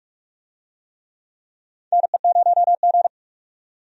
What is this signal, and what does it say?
Morse code, heard as a keyed tone, about 46 wpm, 700 Hz. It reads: NE0G
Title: USA Call Signs - 4 Characters - 45wpm